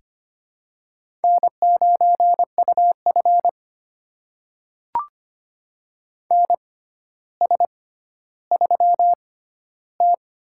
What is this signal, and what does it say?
Morse code, heard as a keyed tone, about 25 wpm, spaced as for 13 wpm, 700 Hz. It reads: N9UF NS3T
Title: USA Call Signs - 4 Characters - 6x Character Spacing - 25wpm